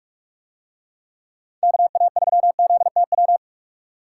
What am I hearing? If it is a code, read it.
Morse code, heard as a keyed tone, about 49 wpm, 700 Hz. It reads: KA3ZTW